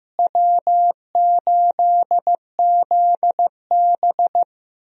Morse code, heard as a keyed tone, about 15 wpm, 700 Hz. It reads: W8ZB
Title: USA Call Signs - 4 Characters - Repeated 3x - 15wpm